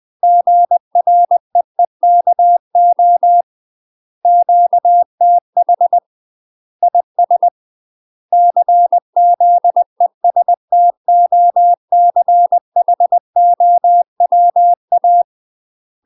Morse code, heard as a keyed tone, about 20 wpm, 700 Hz. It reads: GREEKO QTH IS CZESTOCHOWA
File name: UK MASTER QSOs TRAINING / 20WPM / UK MASTER QSOs 0009